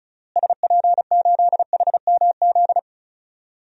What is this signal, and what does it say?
Morse code, heard as a keyed tone, about 34 wpm, 700 Hz. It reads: SP8HMZ
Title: DXCC-100 Call Signs - 6 Characters - Repeated 3x - 35wpm